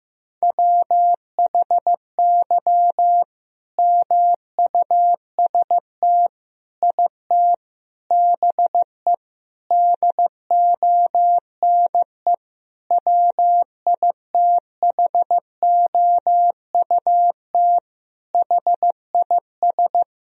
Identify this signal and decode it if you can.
Morse code, heard as a keyed tone, about 15 wpm, 700 Hz. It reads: WHY MUST IT BE DONE WITHOUT HIS
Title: Sentences from Top 1000 Words - Review Part 3 - 15wpm